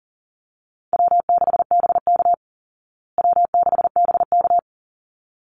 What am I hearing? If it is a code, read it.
Morse code, heard as a keyed tone, about 40 wpm, 700 Hz. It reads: W6BK W6BK
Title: Call Signs Repeated 3x - 40wpm